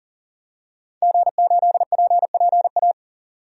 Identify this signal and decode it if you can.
Morse code, heard as a keyed tone, about 40 wpm, 700 Hz. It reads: G8PPA
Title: DXCC-100 Call Signs - 5 Characters - 40wpm